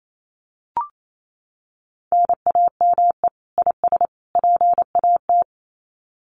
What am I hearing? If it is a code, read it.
Morse code, heard as a keyed tone, about 28 wpm, 700 Hz. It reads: NAME IS PAT
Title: Common QSO Phrases 28wpm